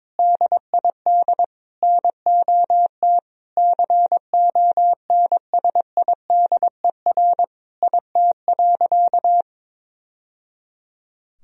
Morse code, heard as a keyed tone, about 22 wpm, 700 Hz. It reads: DID NOT CONSIDER IT.
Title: Sentences from Top 1000 Words - Review Part 4 - 22wpm